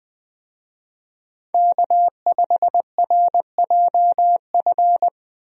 Morse code, heard as a keyed tone, about 20 wpm, 700 Hz. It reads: K5RJF
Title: USA Call Signs - 5 Characters - Repeated 2x - 20wpm